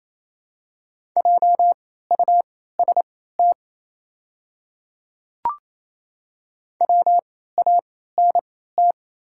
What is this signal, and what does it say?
Morse code, heard as a keyed tone, about 27 wpm, spaced as for 16 wpm, 700 Hz. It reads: JUST WANT
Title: 3x Character Spacing - Top 200 Words - 28wpm